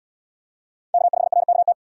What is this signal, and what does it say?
Morse code, heard as a keyed tone, about 68 wpm, 700 Hz. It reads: K5RKT